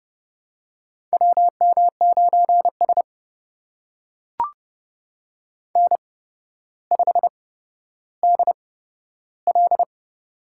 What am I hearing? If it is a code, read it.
Morse code, heard as a keyed tone, about 32 wpm, spaced as for 13 wpm, 700 Hz. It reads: WM9S N5DL
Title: USA Call Signs - 4 Characters - 8x Character Spacing - 30wpm